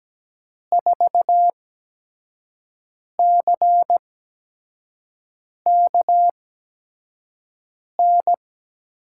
Morse code, heard as a keyed tone, about 17 wpm, spaced as for 5 wpm, 700 Hz. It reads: 4CKN
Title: USA Call Signs - 6 Characters - 8x Character Spacing - 17wpm